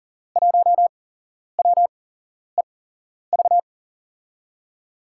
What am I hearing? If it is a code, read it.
Morse code, heard as a keyed tone, about 40 wpm, spaced as for 11 wpm, 700 Hz. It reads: 1WEV
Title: USA Call Signs - 5 Characters - 8x Character Spacing - 40wpm